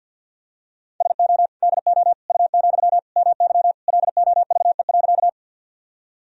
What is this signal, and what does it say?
Morse code, heard as a keyed tone, about 52 wpm, 700 Hz. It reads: SO DO U, MY LOVE.
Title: Sentences from Top 400 Words and Abbreviations 50wpm